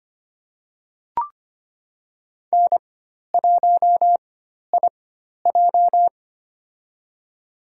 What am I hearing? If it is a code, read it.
Morse code, heard as a keyed tone, about 25 wpm, spaced as for 12 wpm, 700 Hz. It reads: N1IJ
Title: USA Call Signs - 4x Character Spacing - 25wpm